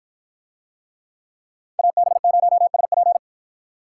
Morse code, heard as a keyed tone, about 53 wpm, 700 Hz. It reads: AB0SP